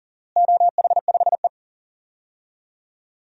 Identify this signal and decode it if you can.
Morse code, heard as a keyed tone, about 37 wpm, 700 Hz. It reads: OH5E
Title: DXCC-100 Call Signs - 4 Characters - Repeated 2x - 40wpm